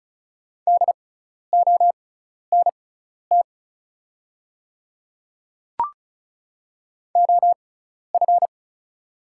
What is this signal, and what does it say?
Morse code, heard as a keyed tone, about 36 wpm, spaced as for 12 wpm, 690 Hz. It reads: DONT OF
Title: 6x Character Spacing - Top 500 Words - 35wpm